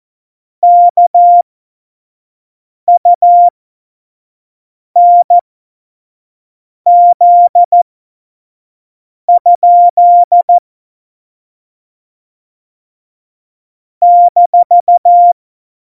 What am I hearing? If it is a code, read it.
Morse code, heard as a keyed tone, about 14 wpm, spaced as for 5 wpm, 700 Hz. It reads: KUNZ? -
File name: Morsekurs / Witze / Witz 27 Juni 17, Tempo 25 F70